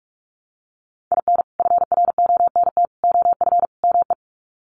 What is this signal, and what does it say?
Morse code, heard as a keyed tone, about 44 wpm, 700 Hz. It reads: IN FRONT OF ME
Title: Sets of 4 Words 45wpm